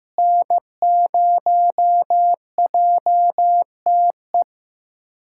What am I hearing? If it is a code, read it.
Morse code, heard as a keyed tone, about 15 wpm, 700 Hz. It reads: N0JTE
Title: USA Callsign DE Callsign - 5 Characters - 15wpm